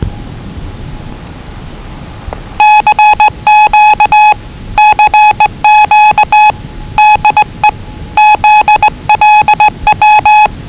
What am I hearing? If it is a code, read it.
Morse code, heard as a keyed tone, about 20 wpm, 840 Hz. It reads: CQ CQ DE ZLW
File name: WellingtonRadioZlwCloseDownOnHfAndMf / 01-Zlw 4277khz 417.5 qrt until 0925z